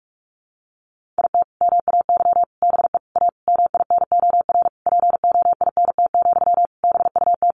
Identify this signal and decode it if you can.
Morse code, heard as a keyed tone, about 44 wpm, 700 Hz. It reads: IT MAY BE A MINOR POINT, BUT
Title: Sentences from Top 1500 Words 45wpm